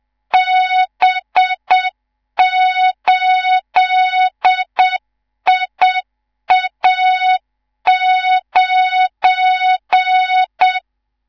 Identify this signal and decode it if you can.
Morse code, heard as a keyed tone, about 7 wpm, 3700 Hz, with noise. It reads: B8IA9
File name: Morze-HA2MN / hang / 19 GYAKORLAT